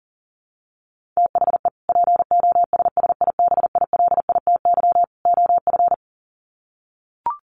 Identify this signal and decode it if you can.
Morse code, heard as a keyed tone, about 40 wpm, 700 Hz. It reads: THE POSSIBILITY OF
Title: Sets of 3 Words 40wpm